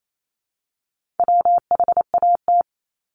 Morse code, heard as a keyed tone, about 28 wpm, 700 Hz. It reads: WHAT
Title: Multi-letter Phonemes Repeated 3x 28wpm